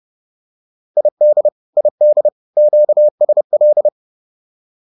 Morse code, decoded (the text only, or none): ID ID QSL